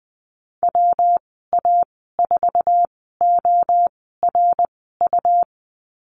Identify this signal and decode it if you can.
Morse code, heard as a keyed tone, about 20 wpm, spaced as for 15 wpm, 700 Hz. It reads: WA4ORU
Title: 147 Lesson 24g - Letter y - Call Signs 20wpm